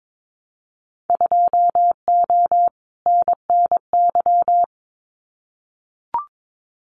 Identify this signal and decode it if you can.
Morse code, heard as a keyed tone, about 22 wpm, 700 Hz. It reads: 2O NNY
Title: Winter Field Day Exchanges 22wpm